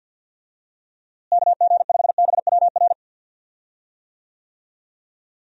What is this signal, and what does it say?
Morse code, heard as a keyed tone, about 51 wpm, 700 Hz. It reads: KG5BWR
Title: USA Call Signs - 6 Characters - Repeated 3x - 50wpm